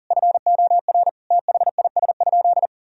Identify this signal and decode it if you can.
Morse code, heard as a keyed tone, about 40 wpm, 700 Hz. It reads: FOR THIS?
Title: Sentences from Top 1000 Words - Review Part 4 - 40wpm